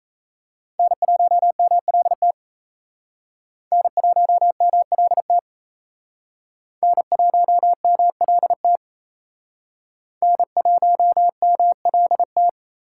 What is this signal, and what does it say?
Morse code, heard as a keyed tone, about 34 wpm, 700 Hz. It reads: N1MLT N1MLT N1MLT N1MLT
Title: Speed Racing - USA 1x3 Call Signs 28wpm